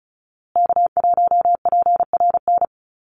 Morse code, heard as a keyed tone, about 34 wpm, 700 Hz. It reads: K1PRN
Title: USA Call Signs Repeated 2x 35wpm